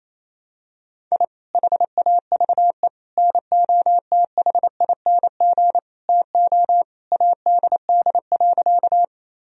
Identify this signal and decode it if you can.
Morse code, heard as a keyed tone, about 28 wpm, 700 Hz. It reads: I HAVE NOTHING TO ADD.
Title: Sentences from Top 1000 Words - Review Part 1 - 28wpm